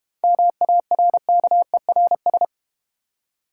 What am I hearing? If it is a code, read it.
Morse code, heard as a keyed tone, about 30 wpm, 690 Hz. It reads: MARKERS
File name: Easy Words - Speed-Racing with Hints - 25wpm